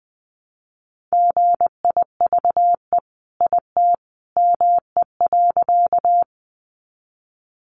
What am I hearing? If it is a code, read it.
Morse code, heard as a keyed tone, about 20 wpm, 700 Hz. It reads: GIVE IT ME.
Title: Sentences from Top 200 Words 20wpm